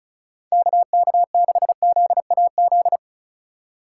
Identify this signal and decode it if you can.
Morse code, heard as a keyed tone, about 34 wpm, 700 Hz. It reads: KK6ZAZ
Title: USA Call Signs - 6 Characters - 2x Character Spacing - 35wpm